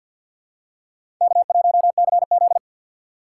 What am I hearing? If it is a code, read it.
Morse code, heard as a keyed tone, about 48 wpm, 700 Hz. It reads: K1CZ
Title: USA Call Signs - 4 Characters - 6x Character Spacing - 50wpm